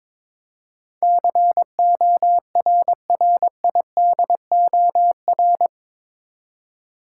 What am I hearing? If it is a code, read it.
Morse code, heard as a keyed tone, about 22 wpm, 700 Hz. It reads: CORRIDOR
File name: practice / 22wpm / Eight Letter Words - 22wpm